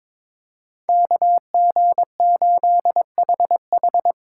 Morse code, heard as a keyed tone, about 22 wpm, 700 Hz. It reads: KG8HH
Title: USA Call Signs - 5 Characters - 8x Character Spacing - 22wpm